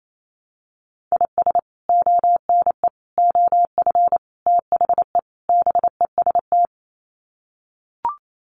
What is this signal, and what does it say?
Morse code, heard as a keyed tone, about 28 wpm, 700 Hz. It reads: IS ONE OF THE BEST
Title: Sets of 5 Words 28wpm